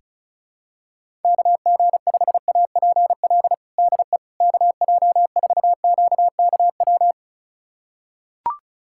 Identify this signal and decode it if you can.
Morse code, heard as a keyed tone, about 34 wpm, 700 Hz. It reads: KG5APL DE KJ4QKW
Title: USA Callsign DE Callsign 35wpm